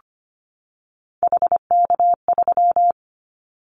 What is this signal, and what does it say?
Morse code, heard as a keyed tone, about 25 wpm, 700 Hz. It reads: HK3
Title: Callsign Prefixes for Top 100 DXCC - 25wpm